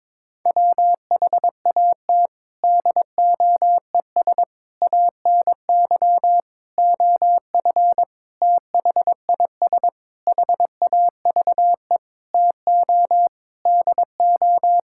Morse code, heard as a keyed tone, about 22 wpm, 700 Hz. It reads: WHAT DOES ANY OF THIS HAVE TO DO
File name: Sentences with 9-Word Phrases - 22wpm